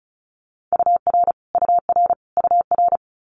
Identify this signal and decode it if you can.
Morse code, heard as a keyed tone, about 36 wpm, 700 Hz. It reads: UR UR UR
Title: Multi-letter Phonemes Repeated 3x 35wpm